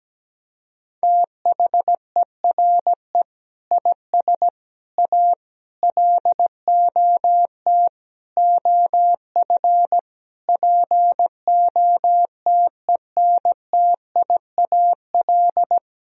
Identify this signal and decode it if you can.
Morse code, heard as a keyed tone, about 17 wpm, 700 Hz. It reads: THERE IS A LOT OF POTENTIAL
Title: Sentences from Top 1000 Words - Review Part 4 - 17wpm